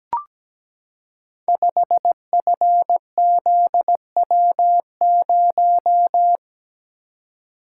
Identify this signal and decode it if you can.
Morse code, heard as a keyed tone, about 17 wpm, 700 Hz. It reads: 5FZW0
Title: Code Groups - 5 Letter:Numbers - 17wpm